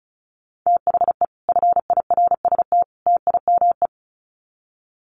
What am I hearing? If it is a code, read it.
Morse code, heard as a keyed tone, about 34 wpm, 700 Hz. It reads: THE FIRST TIME
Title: Sets of 3 Words 35wpm